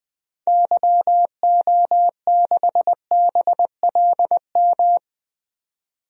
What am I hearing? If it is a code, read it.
Morse code, heard as a keyed tone, about 20 wpm, 700 Hz. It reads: YO6BLM
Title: DXCC-100 Call Signs - 6 Characters - 20wpm